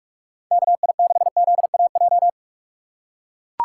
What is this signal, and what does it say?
Morse code, heard as a keyed tone, about 42 wpm, 700 Hz. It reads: KI6ZAJ
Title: USA Call Signs - 6 Characters - 2x Character Spacing - 45wpm